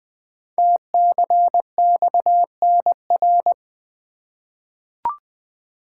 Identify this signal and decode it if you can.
Morse code, heard as a keyed tone, about 20 wpm, 700 Hz. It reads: TCXNR
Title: Code Groups - 5 Letters - 20wpm